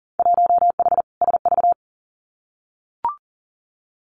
Morse code, heard as a keyed tone, about 42 wpm, 700 Hz. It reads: 1H SV
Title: Winter Field Day Exchanges 40wpm